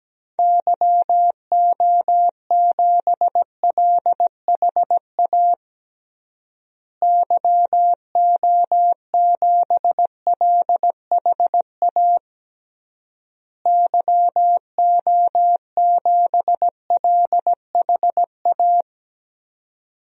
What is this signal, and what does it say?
Morse code, heard as a keyed tone, about 17 wpm, 700 Hz. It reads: YO7LHA YO7LHA YO7LHA
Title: DXCC-100 Call Signs - 6 Characters - Repeated 3x - 17wpm